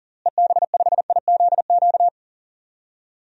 Morse code, heard as a keyed tone, about 37 wpm, 700 Hz. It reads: EB5IZQ
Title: DXCC-100 Call Signs - 6 Characters - Repeated 3x - 40wpm